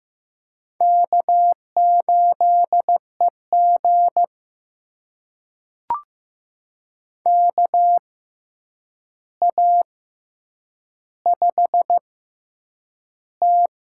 Morse code, heard as a keyed tone, about 15 wpm, spaced as for 8 wpm, 700 Hz. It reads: K8EG KA5T
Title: USA Call Signs - 4 Characters - 6x Character Spacing - 15wpm